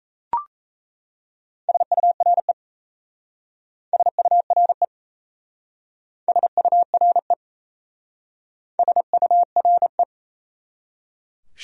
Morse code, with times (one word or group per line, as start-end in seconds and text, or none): 1.7-2.5 SURE
3.9-4.8 SURE
6.3-7.3 SURE
8.8-10.0 SURE